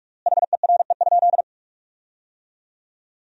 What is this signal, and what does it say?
Morse code, heard as a keyed tone, about 45 wpm, 700 Hz. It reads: HERE?
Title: 4x Word-Spacing - Sentences from Top 200 Words - 45wpm